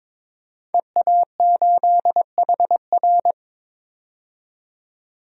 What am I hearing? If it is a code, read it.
Morse code, heard as a keyed tone, about 22 wpm, 700 Hz. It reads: EA8HR
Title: DXCC-100 Call Signs - 5 Characters - 22wpm